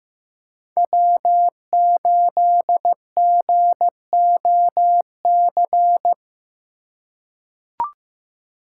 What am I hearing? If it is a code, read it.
Morse code, heard as a keyed tone, about 16 wpm, 700 Hz. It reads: W8GOC
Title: USA Call Signs - 5 Characters - 3x Character Spacing - 15wpm